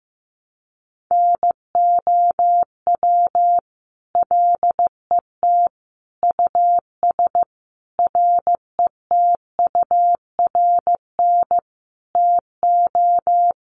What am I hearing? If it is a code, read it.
Morse code, heard as a keyed tone, about 14 wpm, 700 Hz. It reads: NOW LET US RETURN TO
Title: Sentences from Top 800 Words 15wpm